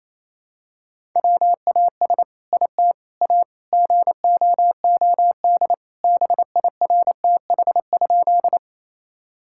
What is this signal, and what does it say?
Morse code, heard as a keyed tone, about 28 wpm, 700 Hz. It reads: WAS IT A GOOD BIRTH?
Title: Sentences from Top 1000 Words - Review Part 5 - 28wpm